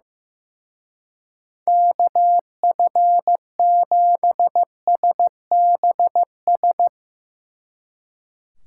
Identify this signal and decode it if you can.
Morse code, heard as a keyed tone, about 15 wpm, 700 Hz. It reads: KF7SBS
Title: Speed Racing - USA 2x3 Call Signs 15wpm